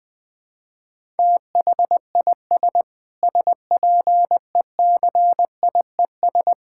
Morse code, heard as a keyed tone, about 21 wpm, 700 Hz. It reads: THIS SPECIES
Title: Sentences with 6-Word Phrases - 20wpm